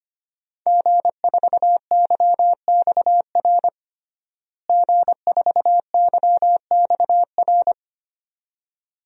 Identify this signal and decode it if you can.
Morse code, heard as a keyed tone, about 25 wpm, 700 Hz. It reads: G4YXR G4YXR
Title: DXCC-100 Call Signs - 5 Characters - Repeated 2x - 25wpm